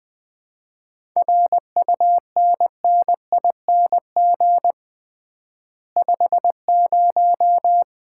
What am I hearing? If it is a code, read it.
Morse code, heard as a keyed tone, about 20 wpm, 700 Hz. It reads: RUNNING 50
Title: Common QSO Phrases - 3x Word-Spacing - 20wpm